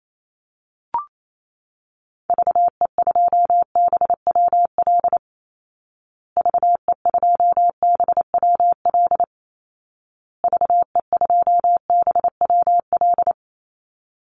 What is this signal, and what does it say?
Morse code, heard as a keyed tone, about 28 wpm, 700 Hz. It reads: VE2BWL VE2BWL VE2BWL
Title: Call Signs Repeated 2x - 28wpm